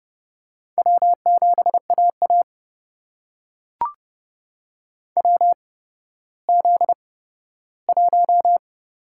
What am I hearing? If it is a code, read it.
Morse code, heard as a keyed tone, about 31 wpm, spaced as for 15 wpm, 700 Hz. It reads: W7AA WZ1
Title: USA Call Signs - 4 Characters - 8x Character Spacing - 30wpm